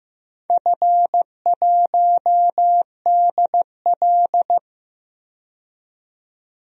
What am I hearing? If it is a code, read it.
Morse code, heard as a keyed tone, about 15 wpm, 700 Hz. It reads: F1DL
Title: DXCC-100 Call Signs - 4 Characters - 15wpm